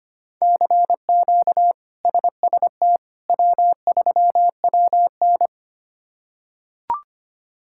Morse code, heard as a keyed tone, about 25 wpm, 700 Hz. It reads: CQ SST W3WN
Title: K1USN SST QSO Practice - 25wpm